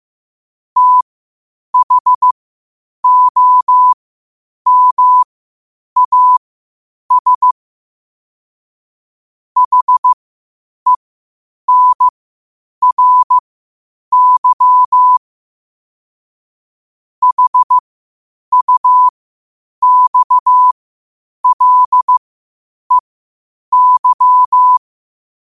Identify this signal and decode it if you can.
Morse code, heard as a keyed tone, about 15 wpm, spaced as for 8 wpm, 1000 Hz. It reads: THOMAS HENRY HUXLEY
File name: qotdfiles / 8wpm